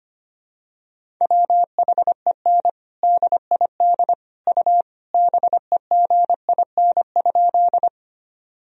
Morse code, heard as a keyed tone, about 25 wpm, 700 Hz. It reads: WHEN DID U BEGIN?
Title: Sentences from Top 300 Words and Abbreviations 25wpm